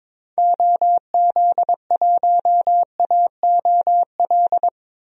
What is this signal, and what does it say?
Morse code, heard as a keyed tone, about 22 wpm, 700 Hz. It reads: OZ1AOL
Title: DXCC-100 Call Signs - 6 Characters - Repeated 2x - 22wpm